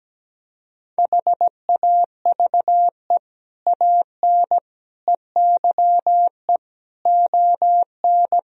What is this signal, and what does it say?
Morse code, heard as a keyed tone, about 17 wpm, 700 Hz. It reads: HAVE AN EYE ON
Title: Sentences from Top 1000 Words - Review Part 1 - 17wpm